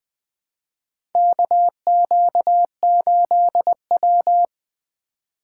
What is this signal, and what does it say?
Morse code, heard as a keyed tone, about 20 wpm, 700 Hz. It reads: KQ8W
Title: USA Call Signs - 4 Characters - 2x Character Spacing - 20wpm